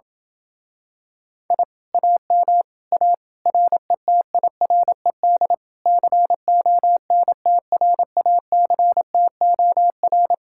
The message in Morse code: I AM A RETIRED CONTRACTOR

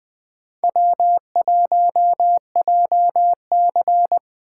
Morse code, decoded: W1JC

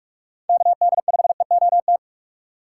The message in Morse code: KD5EOT

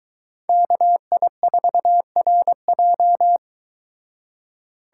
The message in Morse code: KI4RJ

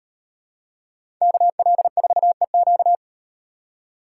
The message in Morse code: KL4EQ

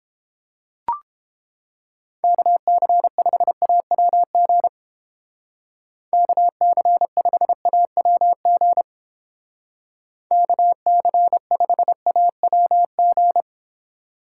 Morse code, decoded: KC5AWG KC5AWG KC5AWG

